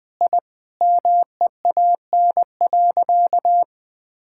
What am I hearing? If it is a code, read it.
Morse code, heard as a keyed tone, about 20 wpm, 700 Hz. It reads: I MEAN.
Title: Sentences with 4-Word Phrases - 20wpm